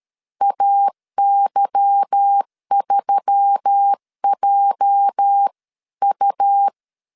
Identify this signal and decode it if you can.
Morse code, heard as a keyed tone, about 13 wpm, 800 Hz. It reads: AY3JU
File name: LEZIONI / 8 YJ45 / 8 YJ 12wpm 2235